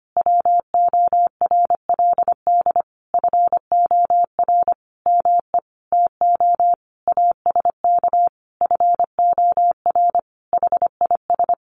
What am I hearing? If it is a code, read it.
Morse code, heard as a keyed tone, about 25 wpm, 700 Hz. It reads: WORLD FOR ME TO ASK FOR HIS
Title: Sentences from Top 1500 Words 25wpm